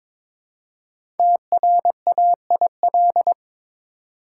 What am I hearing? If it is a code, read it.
Morse code, heard as a keyed tone, about 22 wpm, 700 Hz. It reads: TRAIL
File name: 5 Letter Words - First 3 Letters Given - 22wpm